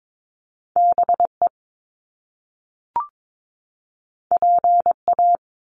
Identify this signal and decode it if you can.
Morse code, heard as a keyed tone, about 22 wpm, 700 Hz. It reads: BE PA
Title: Top Two Letter Combinations in Words 22wpm